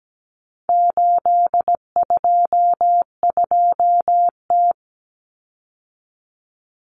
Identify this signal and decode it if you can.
Morse code, heard as a keyed tone, about 17 wpm, 700 Hz. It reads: 822T
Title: Common Cut Numbers - 4-Digit 17wpm